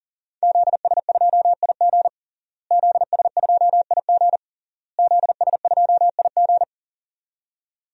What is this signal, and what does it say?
Morse code, heard as a keyed tone, about 40 wpm, 700 Hz. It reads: ZS2IG ZS2IG ZS2IG
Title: DXCC-100 Call Signs - 5 Characters - Repeated 3x - 40wpm